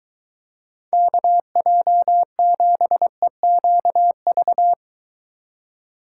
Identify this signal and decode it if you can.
Morse code, heard as a keyed tone, about 22 wpm, 700 Hz. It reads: KJ7EQV